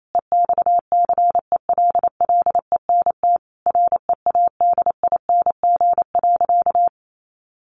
Morse code, heard as a keyed tone, about 28 wpm, 700 Hz. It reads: EXCELLENT READING.
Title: Sentences from Top 1000 Words 28wpm